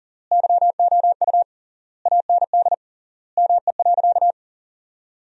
Morse code, heard as a keyed tone, about 37 wpm, 700 Hz. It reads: YOU AND ME.
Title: 3x Word-Spacing - Sentences from Top 500 Words - 40wpm